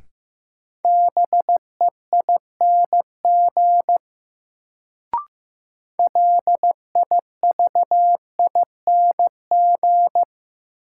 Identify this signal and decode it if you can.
Morse code, heard as a keyed tone, about 15 wpm, 700 Hz. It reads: BEING LIVING
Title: Sets of 4 Words - New Word Drill - 15wpm